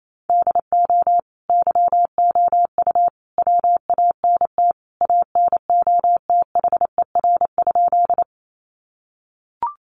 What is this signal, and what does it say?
Morse code, heard as a keyed tone, about 28 wpm, 700 Hz. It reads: DO YOU WANT ANOTHER?
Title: Sentences from Top 600 Words 28wpm